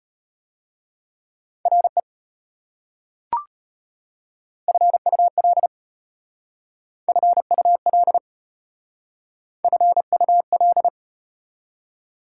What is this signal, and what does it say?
Morse code, heard as a keyed tone, about 33 wpm, 700 Hz. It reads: RE FUL FUL FUL